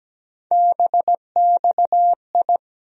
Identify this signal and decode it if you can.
Morse code, heard as a keyed tone, about 17 wpm, 700 Hz. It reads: BXI
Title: USA Call Sign Postfixes - 3 Letters 17wpm